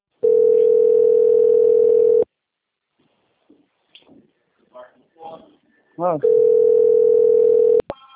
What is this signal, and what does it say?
Ringback tone (North America), 440 + 480 Hz, with speech.
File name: out- - - - - .60745